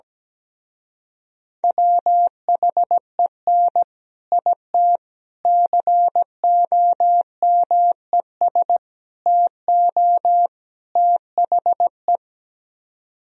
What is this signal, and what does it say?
Morse code, heard as a keyed tone, about 17 wpm, 700 Hz. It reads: WHEN IT COMES TO THE
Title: Speed-Racing - Sets of 5 Words 17wpm